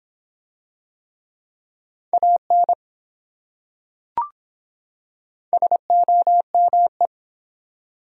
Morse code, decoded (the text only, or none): AN SOME